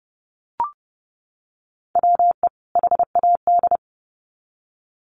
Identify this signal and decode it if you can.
Morse code, heard as a keyed tone, about 30 wpm, 700 Hz. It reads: WE HAD
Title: Sets of 2 Words 30wpm